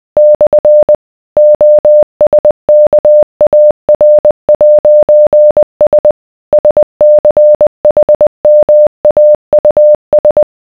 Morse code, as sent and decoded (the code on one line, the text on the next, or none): -..-. / --- ... -.- .- .-. .----. ... / ... -.-. .... -- .- ..- ...
/ OSKAR'S SCHMAUS